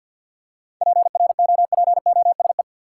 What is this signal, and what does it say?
Morse code, heard as a keyed tone, about 49 wpm, 700 Hz. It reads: PROPOSE
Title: Seven Letter Words - 50wpm